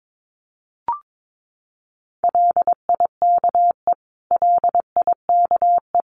Morse code, heard as a keyed tone, about 22 wpm, 700 Hz. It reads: LIKE LIKE
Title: Top 200 Words Repeated 22wpm